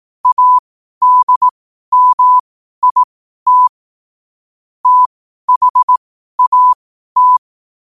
Morse code, heard as a keyed tone, about 18 wpm, spaced as for 12 wpm, 1000 Hz. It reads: ADMIT THAT
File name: qotdfiles / 12wpm